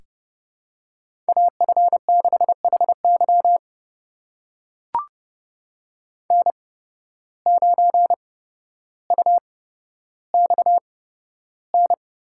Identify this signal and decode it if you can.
Morse code, heard as a keyed tone, about 30 wpm, spaced as for 13 wpm, 700 Hz. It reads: AF6HY N9UXN
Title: USA Call Signs - 5 Characters - 8x Character Spacing - 30wpm